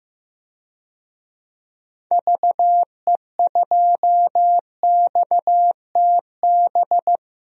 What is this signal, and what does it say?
Morse code, heard as a keyed tone, about 16 wpm, 700 Hz. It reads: VE2XTB
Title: DXCC-100 Call Signs - 6 Characters - 15wpm